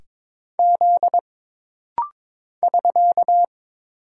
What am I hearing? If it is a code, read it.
Morse code, heard as a keyed tone, about 22 wpm, 700 Hz. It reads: Z <SK>
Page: This rapid-fire timing is geared to push your ICR (Instant Character Recognition) skills.